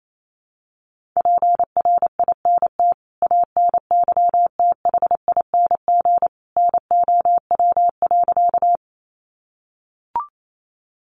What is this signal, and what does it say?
Morse code, heard as a keyed tone, about 28 wpm, 700 Hz. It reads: PRINT ANYTHING NOW.
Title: Sentences from Top 1400 Words 28wpm